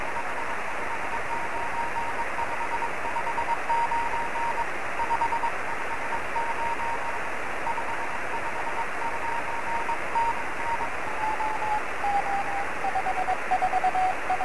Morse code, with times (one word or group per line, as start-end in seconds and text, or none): none